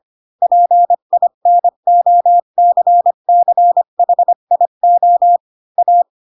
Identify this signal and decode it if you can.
Morse code, heard as a keyed tone, about 25 wpm, 700 Hz. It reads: PINOCCHIO A
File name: Pinocchio / Pinocchio ch8 20 25 ik1hge